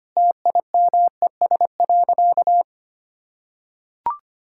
Morse code, heard as a keyed tone, about 25 wpm, 700 Hz. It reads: TIMES.
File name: practice / 25wpm / Sentences with 4-Word Phrases - 25wpm